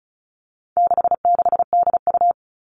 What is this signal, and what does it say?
Morse code, heard as a keyed tone, about 34 wpm, 700 Hz. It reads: 66DU
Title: All Cut Numbers - 4-Digit 35wpm